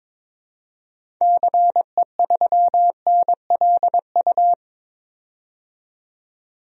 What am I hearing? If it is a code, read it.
Morse code, heard as a keyed tone, about 22 wpm, 700 Hz. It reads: CE3NLU
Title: DXCC-100 Call Signs - 6 Characters - 22wpm